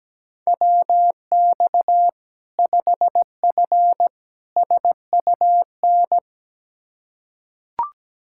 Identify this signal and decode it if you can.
Morse code, heard as a keyed tone, about 17 wpm, 700 Hz. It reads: WX 5F SUN
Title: Practice QSO - Exchange 2 - Info - 17wpm